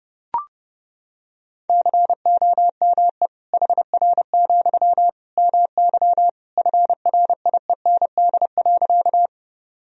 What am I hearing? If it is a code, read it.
Morse code, heard as a keyed tone, about 30 wpm, 700 Hz. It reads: COME HR, MY FRIEND.
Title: Sentences from Top 300 Words and Abbreviations 30wpm